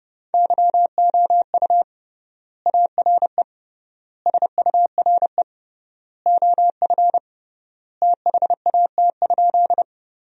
Morse code, heard as a keyed tone, about 32 wpm, 700 Hz. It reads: YOU ARE SURE OF THAT?
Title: 3x Word-Spacing - Sentences from Top 300 Words - 30wpm